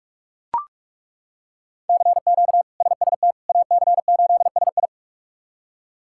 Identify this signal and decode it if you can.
Morse code, heard as a keyed tone, about 42 wpm, 700 Hz. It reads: CQ SST AC8SI
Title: K1USN SST QSO Practice - 45wpm